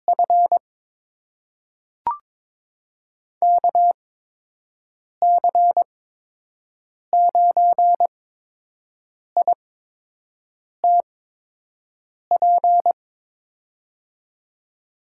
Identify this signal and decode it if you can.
Morse code, heard as a keyed tone, about 22 wpm, spaced as for 6 wpm, 700 Hz. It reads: F KC9ITP
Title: USA Call Signs - 6 Characters - 8x Character Spacing - 22wpm